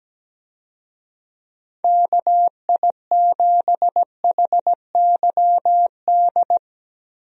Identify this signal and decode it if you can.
Morse code, heard as a keyed tone, about 17 wpm, 700 Hz. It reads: KI7HYD
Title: USA Call Signs - 6 Characters - Repeated 3x - 17wpm